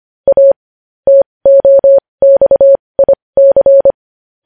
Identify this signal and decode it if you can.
Morse code, heard as a keyed tone, about 25 wpm, spaced as for 20 wpm, 550 Hz. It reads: A TOXIC
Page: news headlines in morse code at 20 wpm